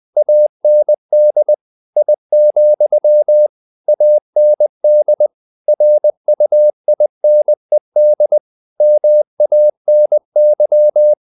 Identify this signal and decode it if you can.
Morse code, heard as a keyed tone, about 20 wpm, 600 Hz. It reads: AND I, AND RUINED MANY
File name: the old man and the sea part 2 20wpm